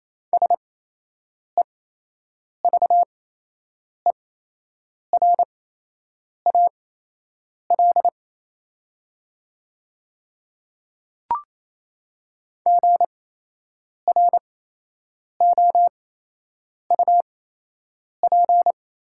Morse code, heard as a keyed tone, about 28 wpm, spaced as for 8 wpm, 700 Hz. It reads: SEVERAL GROUP
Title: 8x Character Spacing - Top 500 Words - 28wpm